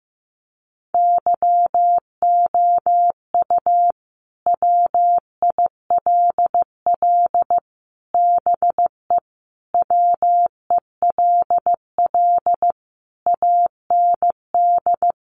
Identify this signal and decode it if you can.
Morse code, heard as a keyed tone, about 15 wpm, 700 Hz. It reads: YOU WILL BE WELL AND
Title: Sentences from Top 1000 Words 15wpm